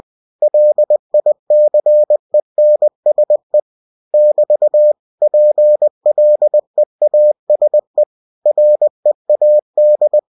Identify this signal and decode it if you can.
Morse code, heard as a keyed tone, about 20 wpm, 600 Hz. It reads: LICENSE <BT> PLEASE READ
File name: cw-books / winnie-the-pooh / winnie-20wpm-page-0112